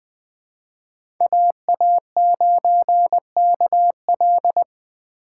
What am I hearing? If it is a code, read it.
Morse code, heard as a keyed tone, about 21 wpm, 700 Hz. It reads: AA9KL
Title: USA Call Signs - 5 Characters - 4x Character Spacing - 20wpm